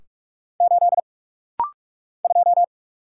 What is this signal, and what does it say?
Morse code, heard as a keyed tone, about 46 wpm, 700 Hz. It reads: 8 2
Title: Single Letter-Number - Rapid-Fire 45wpm